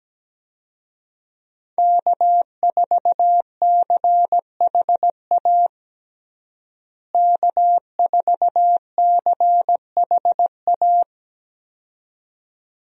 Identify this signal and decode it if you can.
Morse code, heard as a keyed tone, about 17 wpm, 700 Hz. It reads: K4CHA K4CHA
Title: USA Call Signs - 5 Characters - Repeated 2x - 17wpm